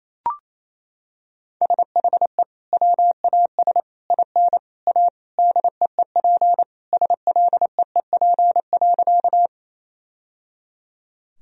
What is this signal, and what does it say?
Morse code, heard as a keyed tone, about 28 wpm, 700 Hz. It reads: SHE WAS IN A DEEP SLEEP.
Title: Sentences from Top 1000 Words - Review Part 3 - 28wpm